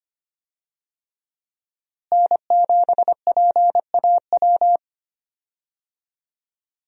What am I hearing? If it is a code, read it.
Morse code, heard as a keyed tone, about 25 wpm, 700 Hz. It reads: N7PAW
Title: USA Call Signs - 5 Characters - Repeated 3x - 25wpm